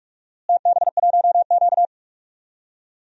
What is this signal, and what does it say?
Morse code, heard as a keyed tone, about 46 wpm, 700 Hz. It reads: TB1Q
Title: DXCC-100 Call Signs - 4 Characters - Repeated 2x - 45wpm